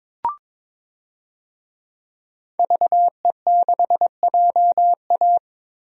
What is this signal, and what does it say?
Morse code, heard as a keyed tone, about 22 wpm, 700 Hz. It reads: VE6JA
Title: DXCC-100 Call Signs - 5 Characters - 22wpm